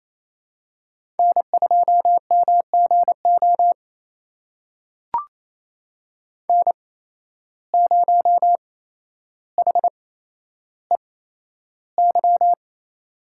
Morse code, heard as a keyed tone, about 28 wpm, spaced as for 12 wpm, 700 Hz. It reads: N2MGO N0HEY
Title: USA Call Signs - 5 Characters - 8x Character Spacing - 28wpm